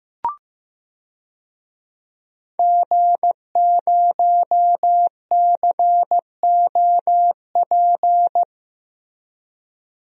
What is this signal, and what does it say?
Morse code, heard as a keyed tone, about 15 wpm, 700 Hz. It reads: G0COP
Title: DXCC-100 Call Signs - 5 Characters - Repeated 3x - 15wpm